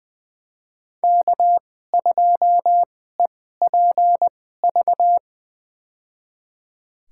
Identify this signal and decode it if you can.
Morse code, heard as a keyed tone, about 21 wpm, spaced as for 15 wpm, 700 Hz. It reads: K2EPV
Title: USA Call Signs - 5 Characters - 2x Character Spacing - 20wpm